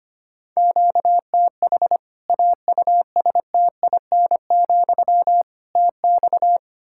Morse code, heard as a keyed tone, about 25 wpm, 700 Hz. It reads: QTH AUSTIN, TX
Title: Practice QSO - Exchange 1 - Beginning - 25wpm